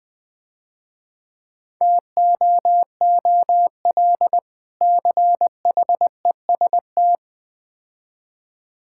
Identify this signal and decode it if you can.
Morse code, heard as a keyed tone, about 20 wpm, 700 Hz. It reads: TOOL CHEST
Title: Difficult Words - Speed-Racing with Hints - 20wpm